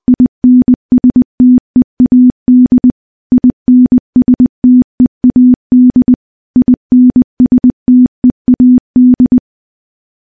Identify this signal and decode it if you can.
Morse code, heard as a keyed tone, about 20 wpm, 250 Hz. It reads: INSTEAD INSTEAD INSTEAD
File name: morse-training-20WPM